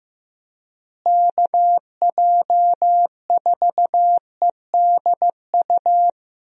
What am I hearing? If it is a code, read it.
Morse code, heard as a keyed tone, about 15 wpm, 700 Hz. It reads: KJ4EDU
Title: USA Call Signs - 6 Characters - Repeated 3x - 15wpm